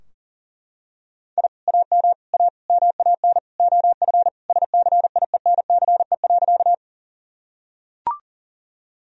Morse code, heard as a keyed tone, about 39 wpm, 700 Hz. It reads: I AM A MAN OF SCIENCE.
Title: Sentences from Top 1000 Words - Review Part 2 - 40wpm